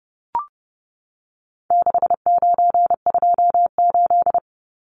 Morse code, heard as a keyed tone, about 30 wpm, 700 Hz. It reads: 6928
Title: Numbers - 4-Digit 30wpm